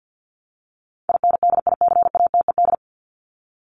Morse code, heard as a keyed tone, about 50 wpm, 700 Hz. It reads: INDICATED
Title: ED Suffix - Top 100 Words with Base Words 50wpm